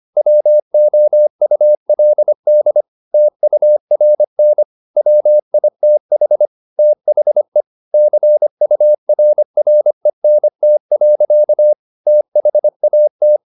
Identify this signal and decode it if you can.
Morse code, heard as a keyed tone, about 25 wpm, 600 Hz. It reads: WOULD TURN WITH THE CURRENT. THAT
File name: the old man and the sea part 1 25wpm